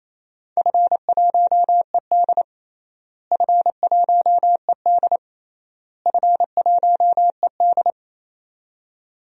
Morse code, heard as a keyed tone, about 28 wpm, 700 Hz. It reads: F1ED F1ED F1ED
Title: DXCC-100 Call Signs - 4 Characters - Repeated 3x - 28wpm